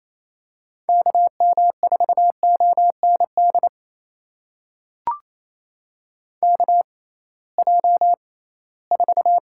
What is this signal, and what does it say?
Morse code, heard as a keyed tone, about 28 wpm, spaced as for 18 wpm, 700 Hz. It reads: KM4OND KJ4
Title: USA Call Signs - 6 Characters - 6x Character Spacing - 28wpm